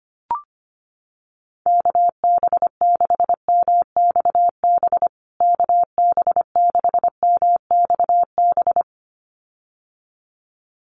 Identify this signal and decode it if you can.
Morse code, heard as a keyed tone, about 25 wpm, 700 Hz. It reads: KB6MXB KB6MXB
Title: USA Call Signs Repeated 2x 25wpm